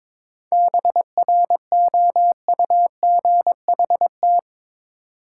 Morse code, heard as a keyed tone, about 21 wpm, 700 Hz. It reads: BROUGHT
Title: Sentences from Top 500 Words - New Word Drill - 22wpm